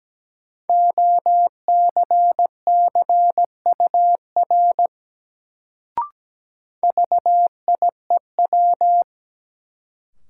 Morse code, heard as a keyed tone, about 17 wpm, 700 Hz. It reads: OCCUR VIEW
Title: Sentences from Top 900 Words - New Word Drill - 17wpm